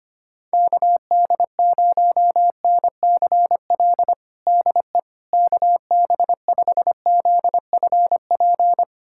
Morse code, heard as a keyed tone, about 25 wpm, 700 Hz. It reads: KD0NCL DE KB5ZFP
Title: USA Callsign DE Callsign - 6 Characters - 25wpm